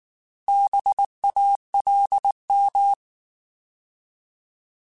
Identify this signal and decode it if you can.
Morse code, heard as a keyed tone, about 19 wpm, 800 Hz. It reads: BALM
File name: Alphabetical / Alphabetical mp3s / Bs @18